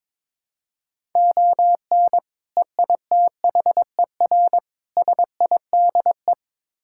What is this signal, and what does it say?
Morse code, heard as a keyed tone, about 22 wpm, 700 Hz. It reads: ON EITHER SIDE
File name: practice / 22wpm / Build A Phrase - 5-Word Phrases - 22wpm